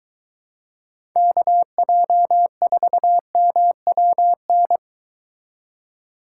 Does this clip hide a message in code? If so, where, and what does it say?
Yes: Morse code from 1.2 to 4.8 s, heard as a keyed tone, about 22 wpm, 700 Hz. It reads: KJ4MWN